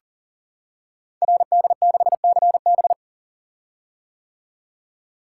Morse code, heard as a keyed tone, about 42 wpm, 700 Hz. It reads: RD6CB
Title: Code Groups - 5 Letter:Numbers - 40wpm